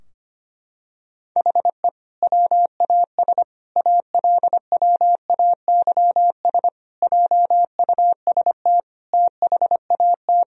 Morse code, heard as a keyed tone, about 25 wpm, 700 Hz. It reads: HE WAS ALWAYS JUST THAT
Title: Sentences from Top 1000 Words - Review Part 4 - 25wpm